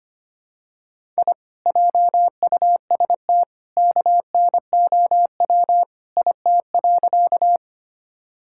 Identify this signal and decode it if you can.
Morse code, heard as a keyed tone, about 25 wpm, 700 Hz. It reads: I JUST KNOW IT.
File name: practice / 25wpm / Sentences with 2-Word Phrases - 25wpm